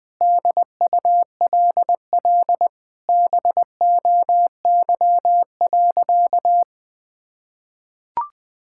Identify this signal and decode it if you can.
Morse code, heard as a keyed tone, about 20 wpm, 700 Hz. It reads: DULL BOY.
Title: Sentences with 7-Word Phrases - Spoken Beginning - 20wpm